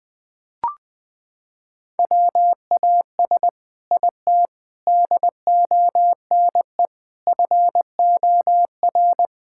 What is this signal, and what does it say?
Morse code, heard as a keyed tone, about 20 wpm, 700 Hz. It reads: WAS IT DONE FOR
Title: Sentences from Top 1000 Words - Review Part 6 - 20wpm